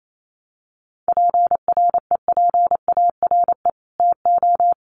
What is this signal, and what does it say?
Morse code, heard as a keyed tone, about 28 wpm, 700 Hz. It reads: PREPARE TO
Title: Sentences from Top 900 Words 28wpm